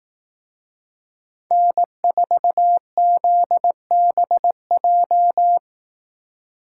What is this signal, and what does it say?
Morse code, heard as a keyed tone, about 18 wpm, 700 Hz. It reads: N4ZBJ
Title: Speed Racing - USA 1x3 Call Signs 15wpm